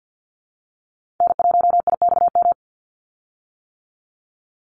Morse code, heard as a keyed tone, about 50 wpm, 700 Hz. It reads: N1IXM
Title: USA Call Signs 50wpm